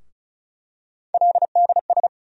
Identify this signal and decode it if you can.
Morse code, heard as a keyed tone, about 34 wpm, 700 Hz. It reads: LDS
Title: USA Call Sign Postfixes - 3 Letters 35wpm